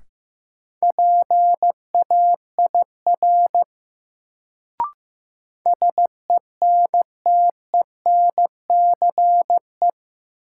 Morse code, heard as a keyed tone, about 15 wpm, 700 Hz. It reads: PAIR SENTENCE
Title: Sentences from Top 1200 Words - New Word Drill Alt - 15wpm